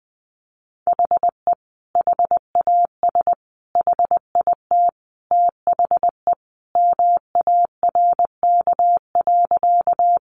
Morse code, heard as a keyed tone, about 20 wpm, 700 Hz. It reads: HE HAS HIT THE MARK.
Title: Sentences from Top 1200 Words 20wpm